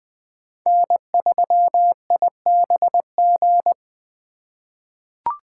Morse code, heard as a keyed tone, about 20 wpm, 700 Hz. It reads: N3IBG
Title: USA Call Signs - 2x Character Spacing - 20wpm